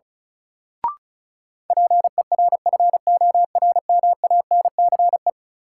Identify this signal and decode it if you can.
Morse code, heard as a keyed tone, about 36 wpm, 700 Hz. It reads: PERFORMANCE
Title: Sentences from Top 1000 Words - New Word Drill - 35wpm